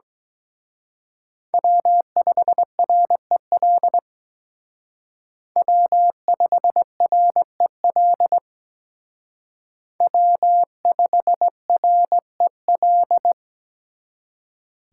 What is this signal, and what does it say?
Morse code, heard as a keyed tone, about 21 wpm, 700 Hz. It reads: W5REL W5REL W5REL
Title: Speed Racing - USA 1x3 Call Signs 17wpm